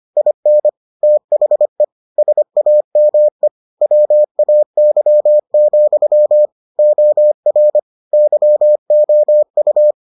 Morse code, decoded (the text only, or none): IN THE SAME WAY, OR YOU